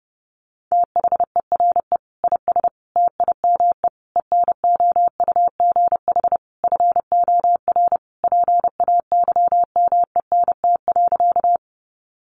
Morse code, heard as a keyed tone, about 28 wpm, 700 Hz. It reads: THERE IS TIME ENOUGH FOR PAYMENT.
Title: Sentences from Top 1200 Words 30wpm